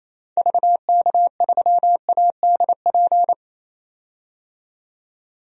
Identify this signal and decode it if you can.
Morse code, heard as a keyed tone, about 28 wpm, 700 Hz. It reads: VK3ADP
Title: DXCC-100 Call Signs - 6 Characters - 28wpm